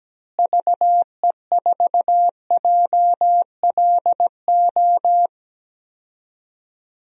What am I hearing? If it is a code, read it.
Morse code, heard as a keyed tone, about 17 wpm, 700 Hz. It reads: VE4JLO
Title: DXCC-100 Call Signs - 6 Characters - Repeated 2x - 17wpm